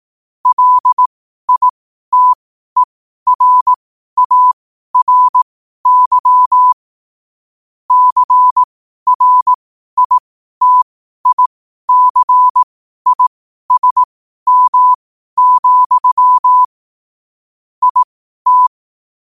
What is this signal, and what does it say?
Morse code, heard as a keyed tone, about 18 wpm, spaced as for 12 wpm, 1000 Hz. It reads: LITERARY CRITICISM, IT
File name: qotdfiles / 12wpm